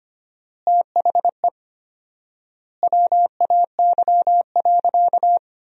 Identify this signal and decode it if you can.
Morse code, heard as a keyed tone, about 25 wpm, 700 Hz. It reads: THE WAY.
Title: 4x Word-Spacing - Sentences from Top 400 Words - 25wpm